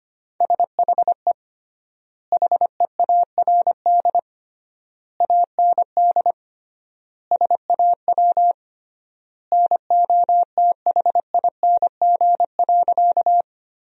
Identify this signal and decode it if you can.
Morse code, heard as a keyed tone, about 25 wpm, 700 Hz. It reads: SHE HEARD AND SAW NOTHING.
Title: 3x Word-Spacing - Sentences from Top 500 Words - 25wpm